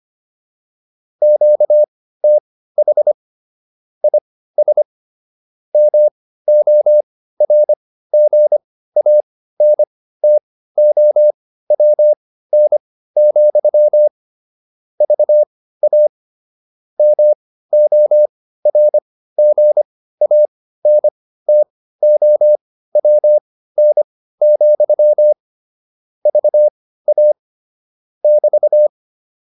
Morse code, heard as a keyed tone, about 25 wpm, spaced as for 15 wpm, 600 Hz. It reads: QTH IS MORGANTOWN, VA MORGANTOWN, VA <BT>
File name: QSO 302 15